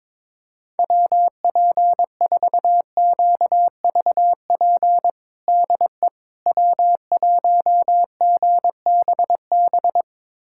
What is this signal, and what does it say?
Morse code, heard as a keyed tone, about 22 wpm, 700 Hz. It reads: WP4QVP DE W1GBB
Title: USA Callsign DE Callsign 22wpm